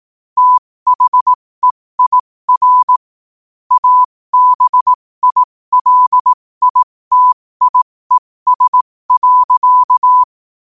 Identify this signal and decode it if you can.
Morse code, heard as a keyed tone, about 18 wpm, spaced as for 15 wpm, 1000 Hz. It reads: THEIR ABILITIES.